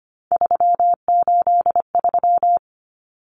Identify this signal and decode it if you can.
Morse code, heard as a keyed tone, about 25 wpm, 700 Hz. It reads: 383
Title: Numbers - 3-Digit 25wpm